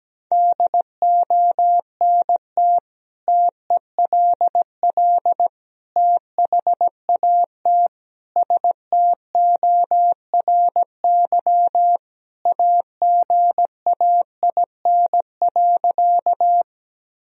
Morse code, heard as a keyed tone, about 17 wpm, 700 Hz. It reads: DONT TELL THAT STORY AGAIN.
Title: Sentences from Top 1000 Words - Review Part 1 - 17wpm